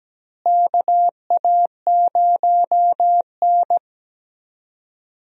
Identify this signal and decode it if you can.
Morse code, heard as a keyed tone, about 17 wpm, 700 Hz. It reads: KA0N
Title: USA Call Signs - 4 Characters - Repeated 3x - 17wpm